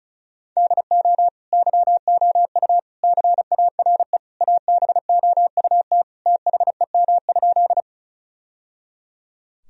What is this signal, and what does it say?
Morse code, heard as a keyed tone, about 34 wpm, 700 Hz. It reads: DO YOU CARE ABOUT THEM?
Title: Sentences from Top 1000 Words - Review Part 1 - 35wpm